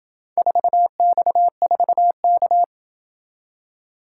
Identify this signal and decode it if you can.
Morse code, heard as a keyed tone, about 28 wpm, 700 Hz. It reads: 4X4K